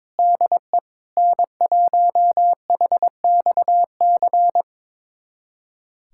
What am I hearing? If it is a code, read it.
Morse code, heard as a keyed tone, about 21 wpm, 700 Hz. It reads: DE N1HXC
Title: USA Callsign DE Callsign - 5 Characters - 22wpm